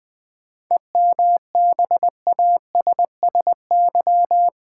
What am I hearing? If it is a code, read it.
Morse code, heard as a keyed tone, about 20 wpm, 700 Hz. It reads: EMBASSY
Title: 7 Letter Words - First 3 Letters Given - 20wpm